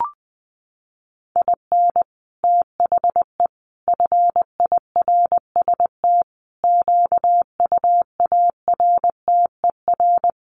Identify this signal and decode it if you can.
Morse code, heard as a keyed tone, about 20 wpm, 700 Hz. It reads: IN THE FIRST QUARTER
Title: Sets of 5 Words Encore2 20wpm